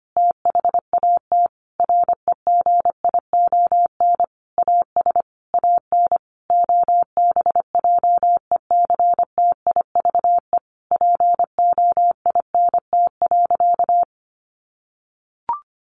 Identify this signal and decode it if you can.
Morse code, heard as a keyed tone, about 25 wpm, 700 Hz. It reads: THAT REGION AS AN OBJECTIVE POINT.
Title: Sentences from Top 1500 Words 25wpm